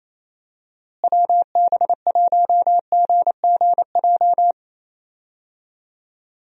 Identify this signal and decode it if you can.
Morse code, heard as a keyed tone, about 28 wpm, 700 Hz. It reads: WB1GGJ
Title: USA Call Signs - 6 Characters - Repeated 2x - 28wpm